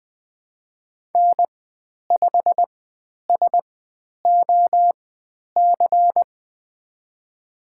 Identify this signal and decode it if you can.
Morse code, heard as a keyed tone, about 20 wpm, spaced as for 10 wpm, 700 Hz. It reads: N5SOC
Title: 095 Lesson 16g - Letter c - Call Signs 20wpm